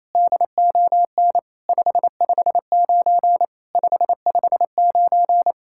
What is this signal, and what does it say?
Morse code, heard as a keyed tone, about 28 wpm, 700 Hz. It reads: DON 559 559
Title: SOTA QSO Practice 28wpm